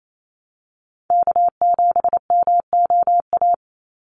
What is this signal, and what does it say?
Morse code, heard as a keyed tone, about 28 wpm, 700 Hz. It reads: K7MOA
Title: Call Signs Repeated 3x - 28wpm